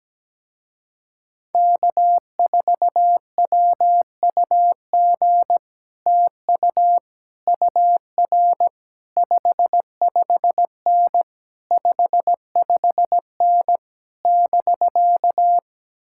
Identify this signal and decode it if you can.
Morse code, heard as a keyed tone, about 17 wpm, 700 Hz. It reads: K4WUG TU UR 55N 55N <BK>
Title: Speed-Racing Common QSO Phrases Encore - 17wpm